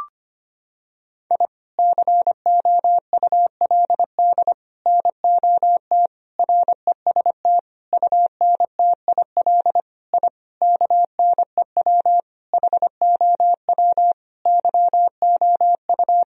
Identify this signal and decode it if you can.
Morse code, heard as a keyed tone, about 25 wpm, 700 Hz. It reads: I COULD NOT REST UNTIL I KNEW HOW YOU
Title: Sentences from Top 1000 Words - Review Part 1 - 25wpm